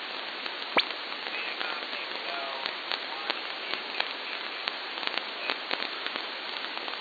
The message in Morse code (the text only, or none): none